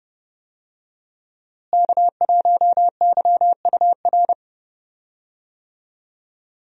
Morse code, heard as a keyed tone, about 32 wpm, 700 Hz. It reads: K1YUR
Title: USA Call Signs - 5 Characters - Repeated 3x - 30wpm